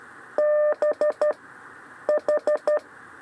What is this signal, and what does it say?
Morse code, heard as a keyed tone, about 13 wpm, 600 Hz. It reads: B H
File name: 6 lezione cw 2